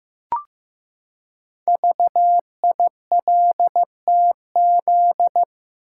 Morse code, heard as a keyed tone, about 15 wpm, 700 Hz. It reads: VILTZ
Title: Code Groups - 5 Letters - 15wpm